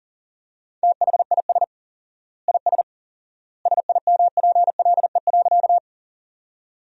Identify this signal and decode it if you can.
Morse code, heard as a keyed tone, about 37 wpm, 700 Hz. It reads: THIS IS SIMPLE.
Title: 4x Word-Spacing - Sentences from Top 400 Words - 40wpm